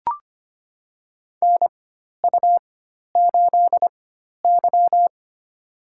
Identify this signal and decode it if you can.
Morse code, heard as a keyed tone, about 25 wpm, spaced as for 12 wpm, 700 Hz. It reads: NU8Y
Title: USA Call Signs - 4x Character Spacing - 25wpm